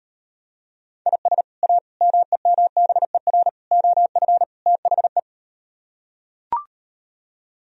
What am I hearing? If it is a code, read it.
Morse code, heard as a keyed tone, about 38 wpm, 700 Hz. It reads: IS A MEMBER OF THE